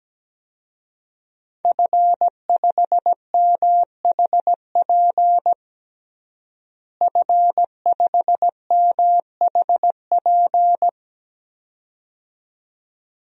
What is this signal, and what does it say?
Morse code, heard as a keyed tone, about 17 wpm, 700 Hz. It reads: F5MHP F5MHP
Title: DXCC-100 Call Signs - 5 Characters - Repeated 2x - 17wpm